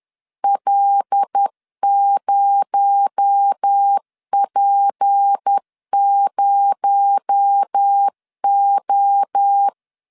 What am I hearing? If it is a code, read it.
Morse code, heard as a keyed tone, about 10 wpm, 800 Hz. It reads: L0P0O
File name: LEZIONI / 6 PKX0 / 6 PKX0 10wpm 2707